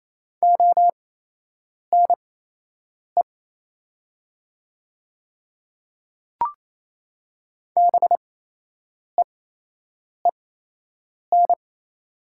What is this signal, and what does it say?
Morse code, heard as a keyed tone, about 28 wpm, spaced as for 8 wpm, 700 Hz. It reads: ONE BEEN
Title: 8x Character Spacing - Top 200 Words - 28wpm